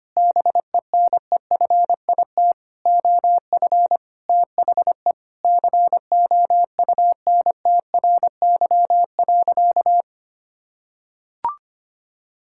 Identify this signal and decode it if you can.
Morse code, heard as a keyed tone, about 25 wpm, 700 Hz. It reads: BENEFIT OF THE COUNTRY.
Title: Sentences from Top 1000 Words - Review Part 6 - 25wpm